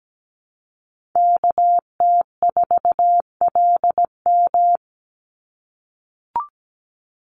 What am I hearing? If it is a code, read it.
Morse code, heard as a keyed tone, about 17 wpm, 700 Hz. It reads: KT4LM
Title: USA Call Signs 17wpm